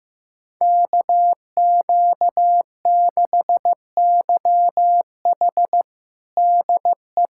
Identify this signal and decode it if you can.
Morse code, heard as a keyed tone, about 16 wpm, 700 Hz. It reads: KQ6YH DE
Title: USA Callsign DE Callsign 15wpm